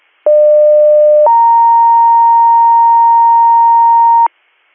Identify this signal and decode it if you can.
Two-tone page, heard as a one-second tone, then a three-second tone, about 600 then 930 Hz, with noise. It is heard from 0.3 to 4.3 s.